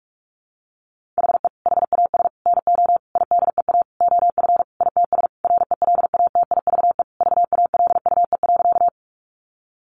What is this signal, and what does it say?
Morse code, heard as a keyed tone, about 44 wpm, 700 Hz. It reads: HE HAS NO IDEA OF ITS RELATIVE VALUE.
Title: Sentences from Top 1400 Words 45wpm